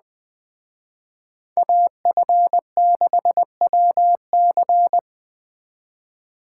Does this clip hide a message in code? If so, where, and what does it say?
Yes: Morse code from 1.6 to 5.0 s, heard as a keyed tone, about 20 wpm, 700 Hz. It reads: AF6WC